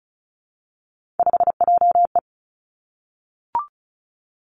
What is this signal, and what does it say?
Morse code, heard as a keyed tone, about 36 wpm, 700 Hz. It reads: 5JE
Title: Three Challenging Letter-Number Combinations 35wpm